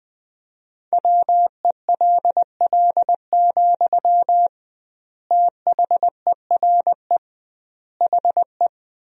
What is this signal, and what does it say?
Morse code, heard as a keyed tone, about 20 wpm, 700 Hz. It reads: WELL, THERE HE
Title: 2x Word-Spacing - Sentences from Top 500 Words - 20wpm